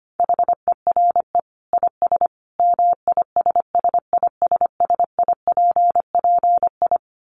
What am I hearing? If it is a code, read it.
Morse code, heard as a keyed tone, about 25 wpm, 700 Hz. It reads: HERE IS MISSISSIPPI
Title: Dit Challenge 25wpm